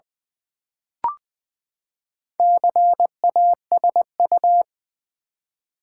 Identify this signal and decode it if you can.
Morse code, heard as a keyed tone, about 20 wpm, 700 Hz. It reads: CASU
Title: Build A Word - 4 Letters At A Time - 8-Letter Words - 20wpm